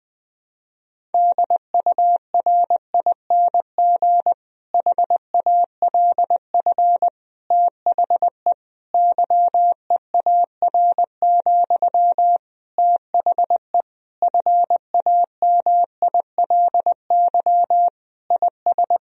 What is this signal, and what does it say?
Morse code, heard as a keyed tone, about 21 wpm, 700 Hz. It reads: DURING HALF THE YEAR, THE FAMILY IS
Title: Sentences from Top 1000 Words - Review Part 2 - 20wpm